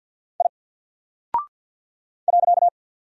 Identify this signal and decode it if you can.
Morse code, heard as a keyed tone, about 48 wpm, 700 Hz. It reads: I .
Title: Single Character (Letter:Number:Punctuation:Prosign) Alt 50wpm